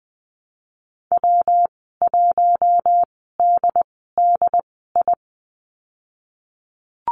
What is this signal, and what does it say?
Morse code, heard as a keyed tone, about 20 wpm, spaced as for 15 wpm, 700 Hz. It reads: W1DDI
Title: 179 Lesson 29g - Slash - Call Signs 20wpm